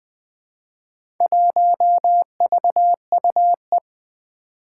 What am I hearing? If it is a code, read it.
Morse code, heard as a keyed tone, about 20 wpm, 700 Hz. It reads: 1VUE
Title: Four Challenging Letter-Number Combinations 20wpm